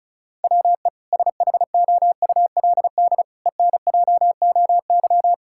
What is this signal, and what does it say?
Morse code, heard as a keyed tone, about 36 wpm, 700 Hz. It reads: WE SHOULD ENJOY
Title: Sentences from Top 1000 Words - Review Part 5 - 35wpm